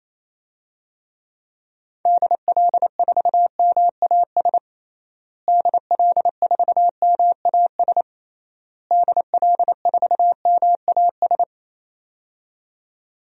Morse code, heard as a keyed tone, about 28 wpm, 700 Hz. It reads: DL4MAS DL4MAS DL4MAS
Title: DXCC-100 Call Signs - 6 Characters - Repeated 3x - 28wpm